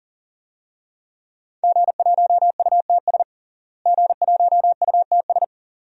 Morse code, heard as a keyed tone, about 40 wpm, 700 Hz. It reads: G1UTS G1UTS
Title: DXCC-100 Call Signs - 5 Characters - Repeated 2x - 40wpm